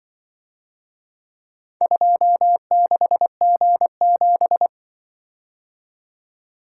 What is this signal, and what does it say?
Morse code, heard as a keyed tone, about 24 wpm, 700 Hz. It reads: 26G7
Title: Speed-Racing - Four Letter-Number Combinations - 20wpm